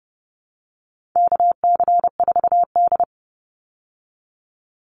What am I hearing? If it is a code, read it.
Morse code, heard as a keyed tone, about 30 wpm, 700 Hz. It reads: KC4D
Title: Call Signs Repeated 3x - 30wpm